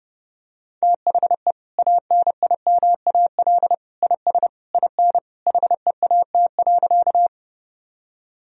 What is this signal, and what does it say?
Morse code, heard as a keyed tone, about 30 wpm, 700 Hz. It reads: THE ANIMAL IS IN HEAT.
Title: Sentences from Top 1000 Words - Review Part 3 - 30wpm